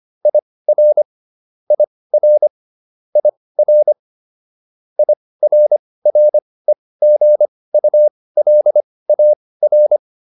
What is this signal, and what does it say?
Morse code, heard as a keyed tone, about 25 wpm, spaced as for 18 wpm, 600 Hz. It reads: IR IR IR IRREGULAR